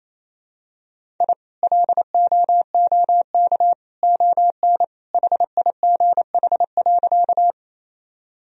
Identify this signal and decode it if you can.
Morse code, heard as a keyed tone, about 28 wpm, 700 Hz. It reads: I LOOK ON HIGH.
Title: Sentences from Top 1000 Words - Review Part 1 - 28wpm